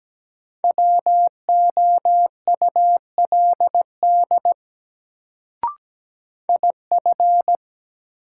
Morse code, heard as a keyed tone, about 17 wpm, 700 Hz. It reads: WOULD IF
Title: Sets of 2 Words - New Word Drill - 17wpm